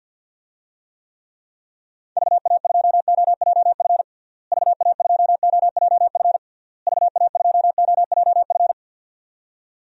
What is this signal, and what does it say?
Morse code, heard as a keyed tone, about 52 wpm, 700 Hz. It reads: VA2OJF VA2OJF VA2OJF
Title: DXCC-100 Call Signs - 6 Characters - Repeated 3x - 50wpm